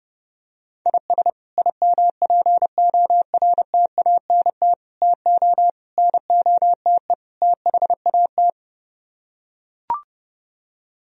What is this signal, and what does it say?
Morse code, heard as a keyed tone, about 29 wpm, 700 Hz. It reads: IS IMPORTANT TO NOTE THAT